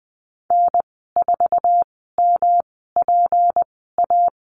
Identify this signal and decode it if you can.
Morse code, heard as a keyed tone, about 20 wpm, spaced as for 15 wpm, 700 Hz. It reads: N4MPA
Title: 179 Lesson 29g - Slash - Call Signs 20wpm